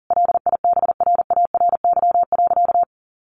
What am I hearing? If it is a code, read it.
Morse code, heard as a keyed tone, about 40 wpm, 700 Hz. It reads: LIBRARY.
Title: Sentences from Top 1400 Words 40wpm